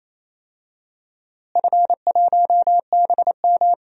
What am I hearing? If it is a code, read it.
Morse code, heard as a keyed tone, about 28 wpm, 700 Hz. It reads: F1BM
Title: DXCC-100 Call Signs - 4 Characters - Repeated 2x - 28wpm